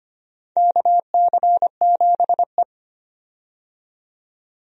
Morse code, heard as a keyed tone, about 25 wpm, 700 Hz. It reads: KC7E
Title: USA Call Signs - 4 Characters - 25wpm